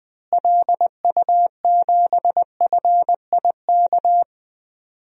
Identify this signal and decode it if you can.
Morse code, heard as a keyed tone, about 21 wpm, 700 Hz. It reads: LU7FIK
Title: DXCC-100 Call Signs - 6 Characters - Repeated 2x - 20wpm